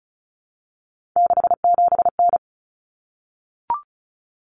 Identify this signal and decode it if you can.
Morse code, heard as a keyed tone, about 34 wpm, 700 Hz. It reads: 67N
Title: Common Cut Numbers - 3-Digit 35wpm